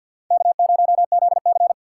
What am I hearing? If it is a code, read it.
Morse code, heard as a keyed tone, about 50 wpm, 700 Hz. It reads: K0ZC
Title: USA Call Signs - 4 Characters - 6x Character Spacing - 50wpm